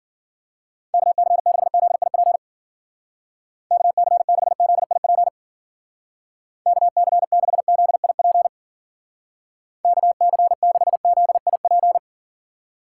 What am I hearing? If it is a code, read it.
Morse code, heard as a keyed tone, about 51 wpm, 700 Hz. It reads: KC6ZIP KC6ZIP KC6ZIP KC6ZIP